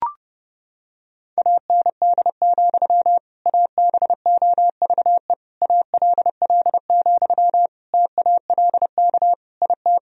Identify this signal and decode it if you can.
Morse code, heard as a keyed tone, about 32 wpm, 700 Hz. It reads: AND, ABOVE ALL, TALK IT
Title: Sentences from Top 1000 Words - Review Part 2 - 30wpm